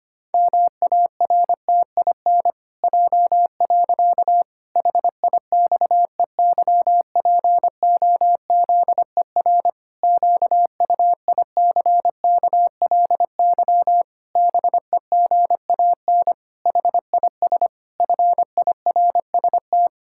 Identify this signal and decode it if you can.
Morse code, heard as a keyed tone, about 25 wpm, 700 Hz. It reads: MARTIN J. HIXEYPOZER QUICKLY BEGAN HIS FIRST
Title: Pangram Challenge v2 - 25wpm